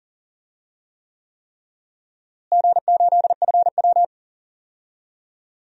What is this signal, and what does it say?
Morse code, heard as a keyed tone, about 40 wpm, 700 Hz. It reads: G8FW
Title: DXCC-100 Call Signs - 4 Characters - Repeated 3x - 40wpm